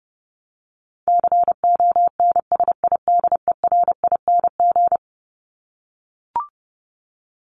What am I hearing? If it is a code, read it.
Morse code, heard as a keyed tone, about 30 wpm, 700 Hz. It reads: CONSIDERING
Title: ING Suffix - Top 100 Words with Base Words 30wpm